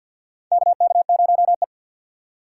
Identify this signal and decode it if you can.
Morse code, heard as a keyed tone, about 53 wpm, 700 Hz. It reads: KK0E
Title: USA Call Signs - 3x Character Spacing - 50wpm